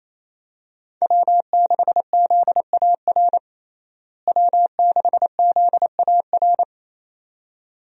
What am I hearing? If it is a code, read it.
Morse code, heard as a keyed tone, about 28 wpm, 700 Hz. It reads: W6ZAR W6ZAR
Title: USA Call Signs - 5 Characters - Repeated 2x - 28wpm